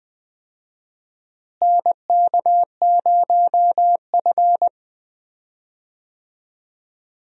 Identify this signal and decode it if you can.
Morse code, heard as a keyed tone, about 21 wpm, 700 Hz. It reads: NK0F
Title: USA Call Signs - 4 Characters - Repeated 3x - 20wpm